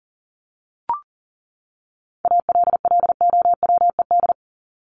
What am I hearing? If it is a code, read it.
Morse code, heard as a keyed tone, about 40 wpm, 700 Hz. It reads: ALLOWED
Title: ED Suffix - Top 100 Words with Base Words 40wpm